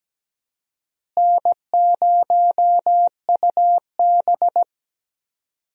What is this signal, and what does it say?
Morse code, heard as a keyed tone, about 17 wpm, 700 Hz. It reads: N0UB
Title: USA Call Signs - 4 Characters - Repeated 2x - 17wpm